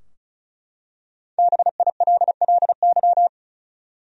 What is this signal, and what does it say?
Morse code, heard as a keyed tone, about 35 wpm, 700 Hz. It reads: BILLY
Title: USA Common Names - Max 5 Letters - 35wpm